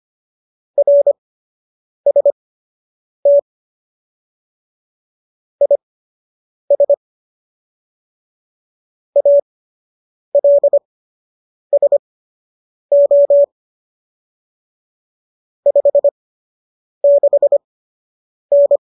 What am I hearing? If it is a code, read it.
Morse code, heard as a keyed tone, about 25 wpm, spaced as for 8 wpm, 570 Hz. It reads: RST IS ALSO 56N
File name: qso006 08